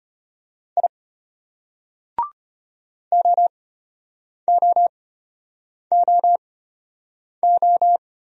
Morse code, heard as a keyed tone, about 33 wpm, spaced as for 8 wpm, 700 Hz. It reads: I OOOO